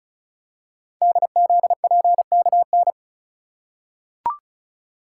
Morse code, heard as a keyed tone, about 36 wpm, 700 Hz. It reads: DZPKN
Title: Code Groups - 5 Letters - 35wpm